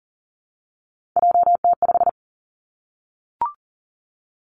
Morse code, heard as a keyed tone, about 42 wpm, 700 Hz. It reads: JT5
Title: Three Challenging Letter-Number Combinations 40wpm